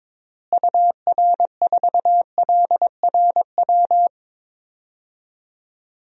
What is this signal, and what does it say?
Morse code, heard as a keyed tone, about 21 wpm, 700 Hz. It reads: UR4LRW
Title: DXCC-100 Call Signs - 6 Characters - Repeated 3x - 22wpm